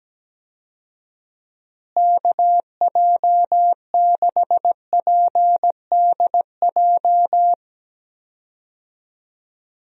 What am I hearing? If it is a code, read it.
Morse code, heard as a keyed tone, about 17 wpm, 700 Hz. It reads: KJ6PDJ
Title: USA Call Signs - 6 Characters - Repeated 2x - 17wpm